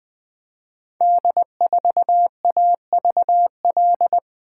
Morse code, heard as a keyed tone, about 20 wpm, 700 Hz. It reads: D4AVL
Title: Code Groups - 5 Letter:Numbers - 20wpm